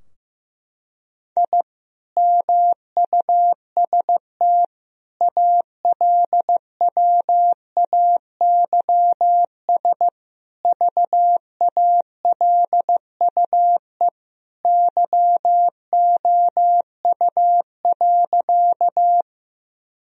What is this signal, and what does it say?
Morse code, heard as a keyed tone, about 15 wpm, 700 Hz. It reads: I MUST ALWAYS VALUE YOU.
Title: Sentences from Top 1000 Words - Review Part 3 - 15wpm